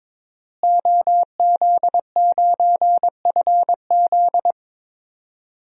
Morse code, heard as a keyed tone, about 22 wpm, 700 Hz. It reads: OZ9FZ
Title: DXCC-100 Call Signs - 5 Characters - Repeated 3x - 22wpm